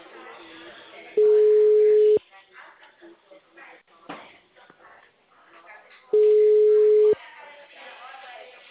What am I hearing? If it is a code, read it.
Ringback tone (Europe, 425 Hz), with speech.